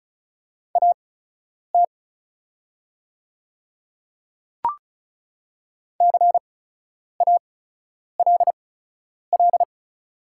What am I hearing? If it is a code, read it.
Morse code, heard as a keyed tone, about 36 wpm, spaced as for 10 wpm, 700 Hz. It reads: AT CALL
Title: 8x Character Spacing - Top 100 Words - 35wpm